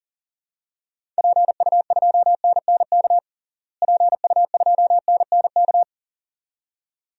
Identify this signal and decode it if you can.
Morse code, heard as a keyed tone, about 40 wpm, 700 Hz. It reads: PU2NNK PU2NNK
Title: DXCC-100 Call Signs - 6 Characters - Repeated 2x - 40wpm